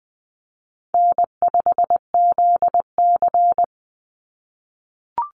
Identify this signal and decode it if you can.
Morse code, heard as a keyed tone, about 20 wpm, 700 Hz. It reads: N5ZC
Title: NA SOTA Chaser Callsigns 20wpm